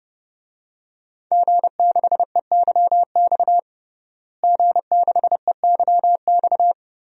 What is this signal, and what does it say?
Morse code, heard as a keyed tone, about 31 wpm, 700 Hz. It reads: G6EYX G6EYX
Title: DXCC-100 Call Signs - 5 Characters - Repeated 3x - 30wpm